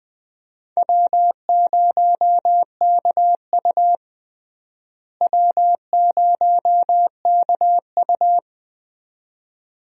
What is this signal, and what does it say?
Morse code, heard as a keyed tone, about 20 wpm, 700 Hz. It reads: W0KU W0KU
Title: USA Call Signs - 4 Characters - Repeated 2x - 20wpm